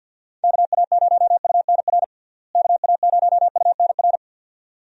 Morse code, heard as a keyed tone, about 53 wpm, 700 Hz. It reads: KA0UNR KA0UNR
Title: USA Call Signs - 6 Characters - Repeated 2x - 50wpm